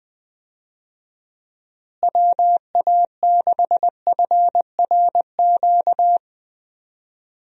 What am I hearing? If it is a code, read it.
Morse code, heard as a keyed tone, about 20 wpm, 700 Hz. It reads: WA6FRQ
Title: USA Call Signs - 6 Characters - Repeated 2x - 20wpm